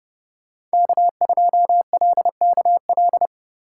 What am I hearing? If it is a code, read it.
Morse code, heard as a keyed tone, about 31 wpm, 700 Hz. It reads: K2LKL